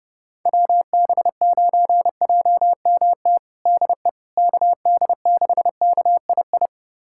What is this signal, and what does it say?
Morse code, heard as a keyed tone, about 30 wpm, 700 Hz. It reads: WB9JMT DE KD6KII
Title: USA Callsign DE Callsign - 6 Characters - 30wpm